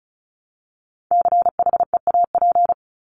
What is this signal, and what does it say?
Morse code, heard as a keyed tone, about 34 wpm, 700 Hz. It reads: CHEAP
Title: Top 1001-1250 Words 35wpm